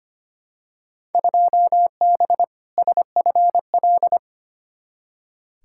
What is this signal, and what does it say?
Morse code, heard as a keyed tone, about 25 wpm, 700 Hz. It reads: 2B SFL
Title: Field Day Exchanges - 25wpm